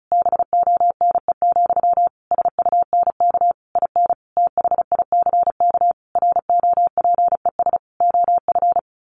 Morse code, heard as a keyed tone, about 34 wpm, 700 Hz. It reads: BONE, SUNK IN THICK ROPES OF
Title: A Sound Of Thunder 35wpm